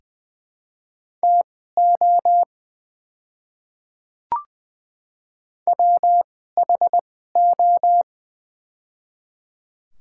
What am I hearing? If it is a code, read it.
Morse code, heard as a keyed tone, about 21 wpm, spaced as for 15 wpm, 700 Hz. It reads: TO WHO
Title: 2x Character Spacing - Top 100 Words - 20wpm